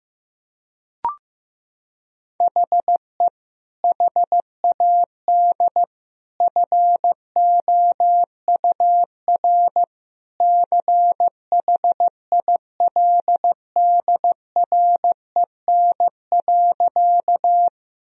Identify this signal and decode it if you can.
Morse code, heard as a keyed tone, about 15 wpm, 700 Hz. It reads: HE HAD FOUR CHILDREN.
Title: Sentences from Top 1000 Words - Review Part 1 - 15wpm